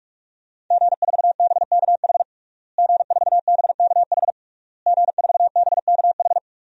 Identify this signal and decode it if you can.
Morse code, heard as a keyed tone, about 46 wpm, 700 Hz. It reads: G4BKH G4BKH G4BKH
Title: DXCC-100 Call Signs - 5 Characters - Repeated 3x - 45wpm